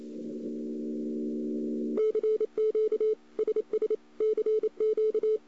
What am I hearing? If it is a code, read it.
Morse code, heard as a keyed tone, about 28 wpm, 430 Hz. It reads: CQ SS CQ
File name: CW / 2008-ss-cw-k5tr-op-n5rz -02:00